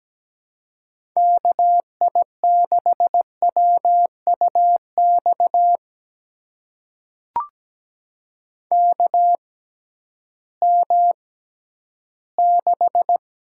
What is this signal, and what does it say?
Morse code, heard as a keyed tone, about 17 wpm, spaced as for 11 wpm, 700 Hz. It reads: KI6WUX KM6
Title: USA Call Signs - 6x Character Spacing - 17wpm